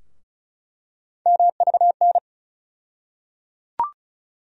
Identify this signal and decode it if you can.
Morse code, heard as a keyed tone, about 34 wpm, 700 Hz. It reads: MVN
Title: USA Call Sign Postfixes - 3 Letters 35wpm